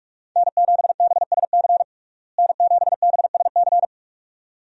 Morse code, heard as a keyed tone, about 46 wpm, 700 Hz. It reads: N7BSC N7BSC
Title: USA Call Signs - 5 Characters - Repeated 2x - 45wpm